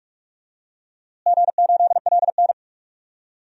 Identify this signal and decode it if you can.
Morse code, heard as a keyed tone, about 46 wpm, 700 Hz. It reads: G8LN
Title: DXCC-100 Call Signs - 4 Characters - Repeated 2x - 45wpm